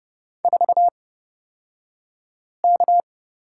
Morse code, heard as a keyed tone, about 30 wpm, spaced as for 5 wpm, 700 Hz. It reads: 4K
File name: Lesson 37d - Letter x - Two Characters 30wpm